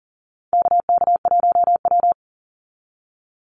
Morse code, heard as a keyed tone, about 40 wpm, 700 Hz. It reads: KK1W
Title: NA SOTA Chaser Callsigns 40wpm